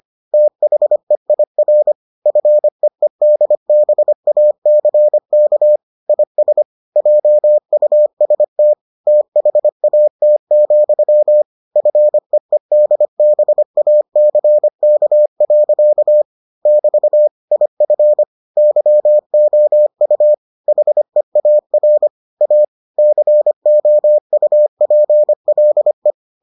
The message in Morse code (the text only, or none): THEIR FEEDBACK IS JUST THAT, FEEDBACK. <BT> IF YOU HEAR A COUPLE